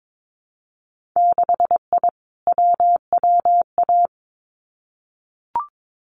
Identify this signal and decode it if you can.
Morse code, heard as a keyed tone, about 22 wpm, 700 Hz. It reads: 6I WWA
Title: Winter Field Day Exchanges Encore 22wpm